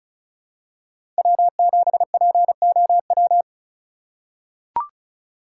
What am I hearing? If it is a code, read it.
Morse code, heard as a keyed tone, about 36 wpm, 700 Hz. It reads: W7POW
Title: USA Call Signs - 5 Characters - 4x Character Spacing - 35wpm